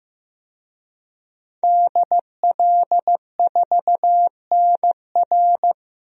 Morse code, heard as a keyed tone, about 15 wpm, 700 Hz. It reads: DL4NR
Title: DXCC-100 Call Signs - 5 Characters - 15wpm